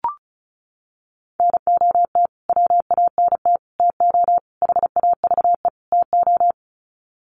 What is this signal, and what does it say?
Morse code, heard as a keyed tone, about 34 wpm, 700 Hz. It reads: NOT WANT TO HAVE TO
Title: Sets of 5 Words Encore2 35wpm